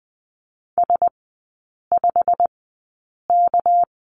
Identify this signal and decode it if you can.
Morse code, heard as a keyed tone, about 20 wpm, spaced as for 8 wpm, 690 Hz. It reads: S5K
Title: 198 Lesson 32e - Letter k - Three Characters 20wpm